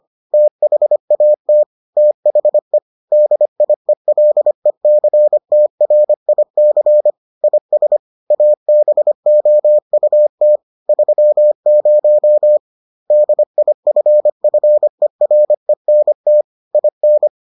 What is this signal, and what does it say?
Morse code, heard as a keyed tone, about 25 wpm, 600 Hz. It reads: THAT THE DIELECTRIC IS ABOUT 30 DIFFERENT IN